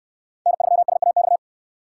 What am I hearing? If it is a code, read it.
Morse code, heard as a keyed tone, about 69 wpm, 700 Hz. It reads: N3SAX